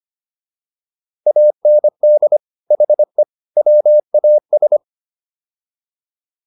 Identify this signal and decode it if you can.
Morse code, heard as a keyed tone, about 25 wpm, 600 Hz. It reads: AND HE WAS